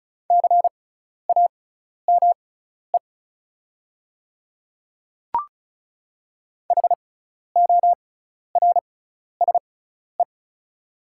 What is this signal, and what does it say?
Morse code, heard as a keyed tone, about 34 wpm, spaced as for 12 wpm, 690 Hz. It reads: CAME HORSE
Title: 6x Character Spacing - Top 500 Words - 35wpm